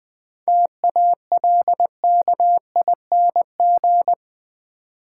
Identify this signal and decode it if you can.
Morse code, heard as a keyed tone, about 20 wpm, 700 Hz. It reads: TALKING
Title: Sets of 2 Words - New Word Drill - 20wpm